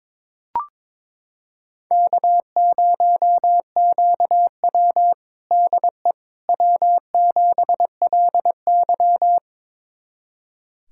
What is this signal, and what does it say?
Morse code, heard as a keyed tone, about 21 wpm, 700 Hz. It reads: K0QW DE W7LY
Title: USA Callsign DE Callsign - 4 Characters - 22wpm